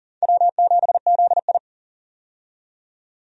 Morse code, heard as a keyed tone, about 40 wpm, 700 Hz. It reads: W7ZI
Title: USA Call Signs - 4 Characters - 40wpm